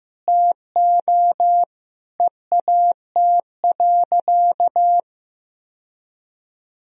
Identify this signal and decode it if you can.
Morse code, heard as a keyed tone, about 15 wpm, 700 Hz. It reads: TO EAT.
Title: Sentences from Top 1000 Words - Review Part 3 - 15wpm